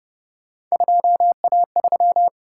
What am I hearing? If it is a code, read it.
Morse code, heard as a keyed tone, about 29 wpm, 700 Hz. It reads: 2A3